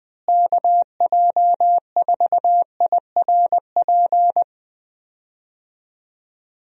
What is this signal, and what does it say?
Morse code, heard as a keyed tone, about 20 wpm, 700 Hz. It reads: KJ4IRP
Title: USA Call Signs - 6 Characters - Repeated 3x - 20wpm